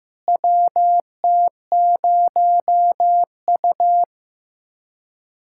Morse code, heard as a keyed tone, about 15 wpm, 700 Hz. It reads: WT0U
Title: USA Callsign DE Callsign - 4 Characters - 15wpm